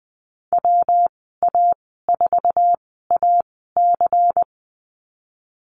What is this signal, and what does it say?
Morse code, heard as a keyed tone, about 20 wpm, spaced as for 14 wpm, 700 Hz. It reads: WA4AC
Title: 140 Lesson 23g - Letter f - Call Signs 20wpm